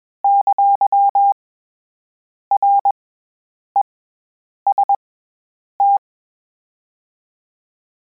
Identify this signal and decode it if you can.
Morse code, heard as a keyed tone, about 21 wpm, 800 Hz. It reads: ! R E S T
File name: Combinations / Combinations-mp3s / Combinations@20